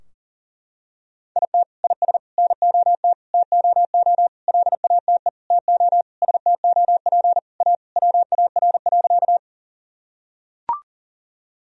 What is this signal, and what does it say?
Morse code, heard as a keyed tone, about 40 wpm, 700 Hz. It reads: IT IS NOT TOO LATE TO STOP A WAR.
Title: Sentences from Top 1000 Words - Review Part 2 - 40wpm